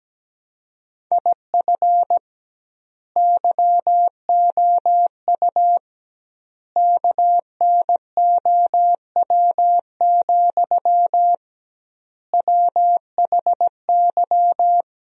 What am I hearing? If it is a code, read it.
Morse code, heard as a keyed tone, about 17 wpm, 700 Hz. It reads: IF YOU KNOW, WHY
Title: 2x Word-Spacing - Sentences from Top 300 Words - 17wpm